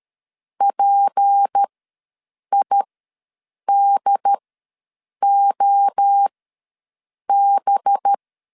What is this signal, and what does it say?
Morse code, heard as a keyed tone, about 13 wpm, spaced as for 7 wpm, 800 Hz. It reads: PIDOB
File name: LEZIONI / 6 PKX0 / 6 P ws 12wpm 3117